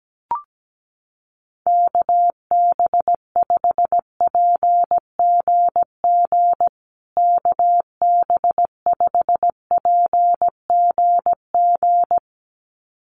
Morse code, heard as a keyed tone, about 17 wpm, 700 Hz. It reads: KB5PGG KB5PGG
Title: USA Call Signs Repeated 2x 17wpm